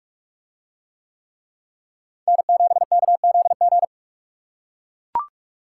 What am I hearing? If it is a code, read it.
Morse code, heard as a keyed tone, about 44 wpm, 700 Hz. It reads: N7KZG